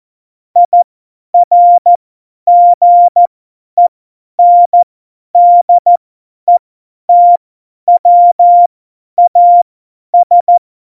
Morse code, heard as a keyed tone, about 14 wpm, spaced as for 10 wpm, 700 Hz. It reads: IRGENDETWAS
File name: Morsekurs / Witze / Witz 27 March 2018, 50 F70